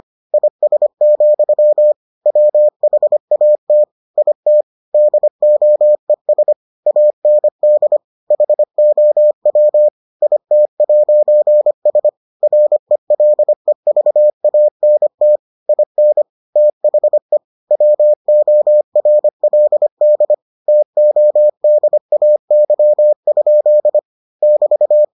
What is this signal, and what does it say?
Morse code, heard as a keyed tone, about 25 wpm, 600 Hz. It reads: IS, WHAT IT DOES AND HOW IT'S RELEVANT IN THE WORLD TODAY? <BT>